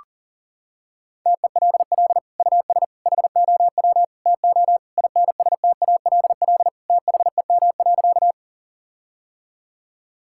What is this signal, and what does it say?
Morse code, heard as a keyed tone, about 40 wpm, 700 Hz. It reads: TELL US HOW TO INSTALL THEM.
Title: Sentences from Top 1600 Words 40wpm